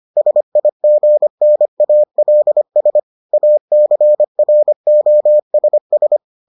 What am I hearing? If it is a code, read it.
Morse code, heard as a keyed tone, about 25 wpm, 600 Hz. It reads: SIGNALS ACROSS